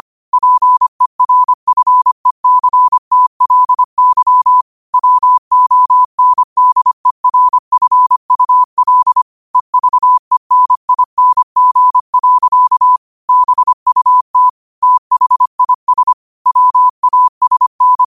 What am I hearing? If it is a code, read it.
Morse code, heard as a keyed tone, about 25 wpm, 1000 Hz. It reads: PERFECTLY WONDERFUL EVENING. BUT THIS WASN
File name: qotdfiles / 25wpm